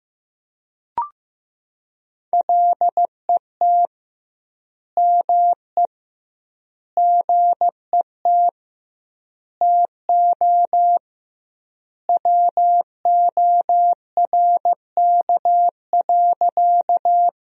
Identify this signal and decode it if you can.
Morse code, heard as a keyed tone, about 15 wpm, 700 Hz. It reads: LET ME GET TO WORK.
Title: 2x Word-Spacing - Sentences from Top 300 Words - 15wpm